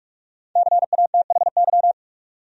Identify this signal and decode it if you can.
Morse code, heard as a keyed tone, about 46 wpm, 700 Hz. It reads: CATHY
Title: USA Common Names - Max 5 Letters - 45wpm